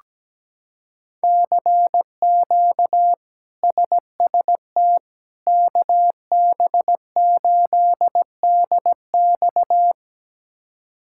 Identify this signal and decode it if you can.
Morse code, heard as a keyed tone, about 17 wpm, 700 Hz. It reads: CQ SST KB8DX
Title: K1USN SST QSO Practice - 17wpm